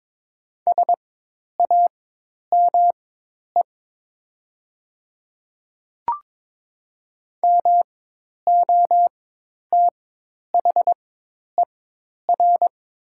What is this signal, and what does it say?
Morse code, heard as a keyed tone, about 22 wpm, spaced as for 10 wpm, 700 Hz. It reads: SAME MOTHER
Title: 4x Character Spacing - Top 300 Words - 22wpm